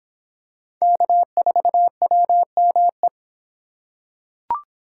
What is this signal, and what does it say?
Morse code, heard as a keyed tone, about 26 wpm, 700 Hz. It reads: K4WME